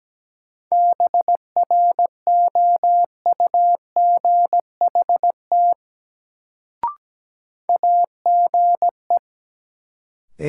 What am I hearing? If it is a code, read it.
Morse code, heard as a keyed tone, about 17 wpm, 700 Hz. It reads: BROUGHT AGE
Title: Sentences from Top 500 Words - New Word Drill - 17wpm